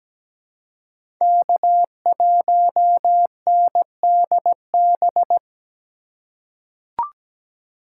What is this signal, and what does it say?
Morse code, heard as a keyed tone, about 17 wpm, 700 Hz. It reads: K1NDB
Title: USA Call Signs - 5 Characters - 3x Character Spacing - 17wpm